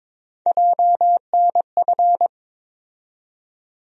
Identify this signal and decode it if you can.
Morse code, heard as a keyed tone, about 22 wpm, 700 Hz. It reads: JNF
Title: USA Call Sign Postfixes - 3 Letters 22wpm